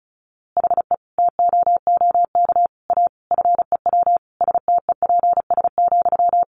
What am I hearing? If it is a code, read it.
Morse code, heard as a keyed tone, about 34 wpm, 700 Hz. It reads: HE TOOK A FEW STEPS,
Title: A Sound Of Thunder 35wpm